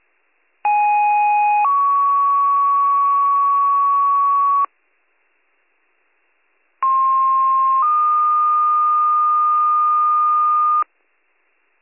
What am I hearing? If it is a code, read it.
Two-tone page, heard as a one-second tone, then a three-second tone, about 830 then 1120 Hz, about 1060 then 1220 Hz. It is heard from 0.6 to 10.8 s.